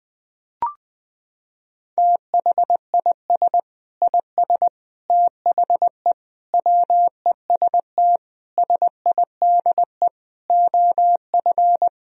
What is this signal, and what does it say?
Morse code, heard as a keyed tone, about 20 wpm, 700 Hz. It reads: THIS IS THE WEST SIDE OF
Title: Sentences from Top 1000 Words - Review Part 2 - 20wpm